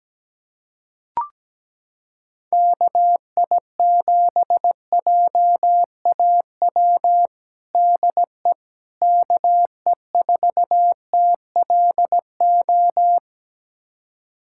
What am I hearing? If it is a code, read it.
Morse code, heard as a keyed tone, about 17 wpm, 700 Hz. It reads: KI7JAW DE KE4TLO
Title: USA Callsign DE Callsign 17wpm